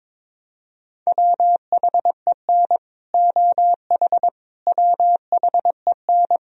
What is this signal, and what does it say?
Morse code, heard as a keyed tone, about 21 wpm, 700 Hz. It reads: WHEN OH WHEN
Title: Sentences from Top 1000 Words - Review Part 2 - 22wpm